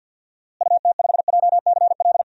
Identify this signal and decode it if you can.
Morse code, heard as a keyed tone, about 51 wpm, 700 Hz. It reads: UT5JCL
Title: DXCC-100 Call Signs - 6 Characters - Repeated 2x - 50wpm